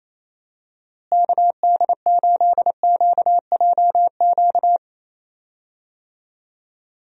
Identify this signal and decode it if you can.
Morse code, heard as a keyed tone, about 28 wpm, 700 Hz. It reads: KD8QJQ
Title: USA Call Signs - 6 Characters - Repeated 2x - 28wpm